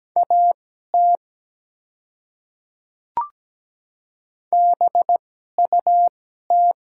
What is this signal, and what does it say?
Morse code, heard as a keyed tone, about 17 wpm, spaced as for 12 wpm, 700 Hz. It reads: AT BUT
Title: 2x Character Spacing - Top 300 Words - 17wpm